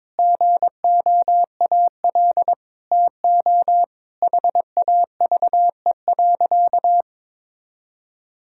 Morse code, heard as a keyed tone, about 22 wpm, 700 Hz. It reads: GOAL TO HAVE.
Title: Sentences from Top 1000 Words - Review Part 5 - 22wpm